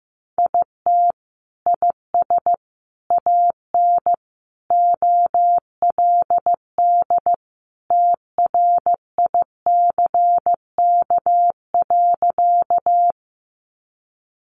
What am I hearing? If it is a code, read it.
Morse code, heard as a keyed tone, about 14 wpm, 700 Hz. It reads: IT IS AN OLD TRICK.
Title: Sentences from Top 1500 Words 15wpm